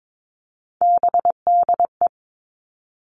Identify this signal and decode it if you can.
Morse code, heard as a keyed tone, about 22 wpm, 700 Hz. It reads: BDE
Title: Three Challenging Letter-Number Combinations 22wpm